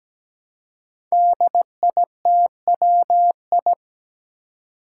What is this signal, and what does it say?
Morse code, heard as a keyed tone, about 17 wpm, 700 Hz. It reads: DITWI
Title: Five Challenging Letter-Number Combinations 17wpm